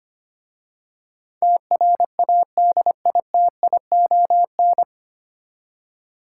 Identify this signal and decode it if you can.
Morse code, heard as a keyed tone, about 25 wpm, 690 Hz. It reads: TRADITION
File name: practice / 25wpm / 9 Letter Words - First 3 Letters Given - 25wpm